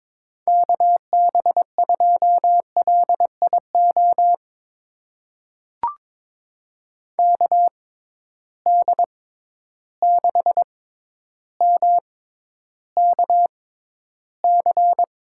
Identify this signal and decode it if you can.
Morse code, heard as a keyed tone, about 22 wpm, spaced as for 11 wpm, 700 Hz. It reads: KB2LIO KD6MKC
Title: USA Call Signs - 6 Characters - 6x Character Spacing - 22wpm